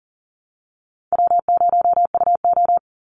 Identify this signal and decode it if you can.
Morse code, heard as a keyed tone, about 40 wpm, 700 Hz. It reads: W0UO
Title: Call Signs Repeated 2x - 40wpm